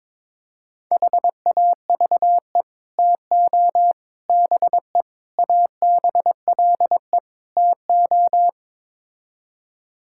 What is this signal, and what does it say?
Morse code, heard as a keyed tone, about 22 wpm, 700 Hz. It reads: HAVE TO BE ABLE TO
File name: practice / 22wpm / Build A Phrase - 5-Word Phrases - 22wpm